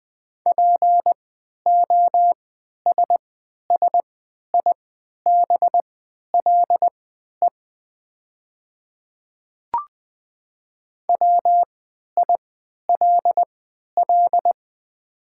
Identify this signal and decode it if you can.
Morse code, heard as a keyed tone, about 20 wpm, spaced as for 11 wpm, 700 Hz. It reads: POSSIBLE WILL
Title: 3x Character Spacing - Top 500 Words - 20wpm